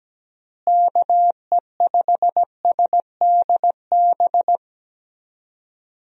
Morse code, heard as a keyed tone, about 17 wpm, 700 Hz. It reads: KE5SDB
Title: USA Call Signs - 6 Characters - Repeated 2x - 17wpm